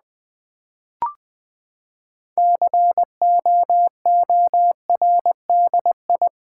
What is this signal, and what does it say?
Morse code, heard as a keyed tone, about 20 wpm, 700 Hz. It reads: COORDI
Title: Build A Word - 6 Letters At A Time - 12-Letter Words - 20wpm